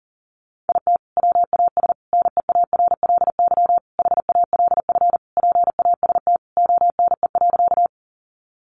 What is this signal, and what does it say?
Morse code, heard as a keyed tone, about 40 wpm, 700 Hz. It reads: IT WAS NEARLY HALF PAST ONE.
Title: Sentences from Top 1300 Words 40wpm